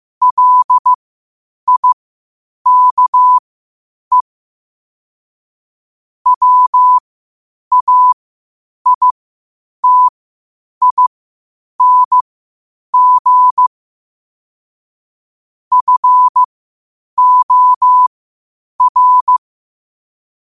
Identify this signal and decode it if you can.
Morse code, heard as a keyed tone, about 15 wpm, spaced as for 8 wpm, 1000 Hz. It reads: LIKE WAITING FOR
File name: qotdfiles / 8wpm